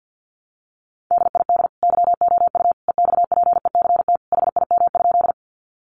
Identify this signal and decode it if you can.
Morse code, heard as a keyed tone, about 50 wpm, 700 Hz. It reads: DID YOU EXPECT HIM?
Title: Sentences from Top 1100 Words 50wpm